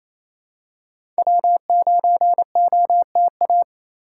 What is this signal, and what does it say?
Morse code, heard as a keyed tone, about 28 wpm, 700 Hz. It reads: W9OTA
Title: USA Call Signs - 2x Character Spacing - 28wpm